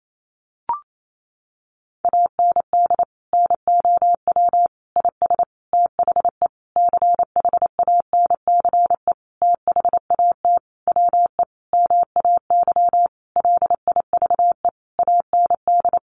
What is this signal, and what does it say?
Morse code, heard as a keyed tone, about 28 wpm, 700 Hz. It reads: AND NOW IS THE CHANCE THAT WE MAY LIVE AND
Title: Sentences from Top 800 Words 28wpm